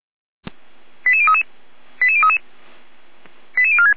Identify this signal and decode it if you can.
Five-tone selective call (ZVEI), heard as quick runs of five tones, tones of about 70 ms. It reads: 88033 88033 88046